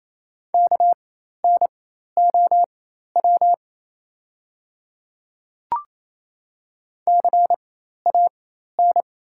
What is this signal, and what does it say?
Morse code, heard as a keyed tone, about 28 wpm, spaced as for 13 wpm, 700 Hz. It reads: KNOW CAN
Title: 4x Character Spacing - Top 100 Words - 28wpm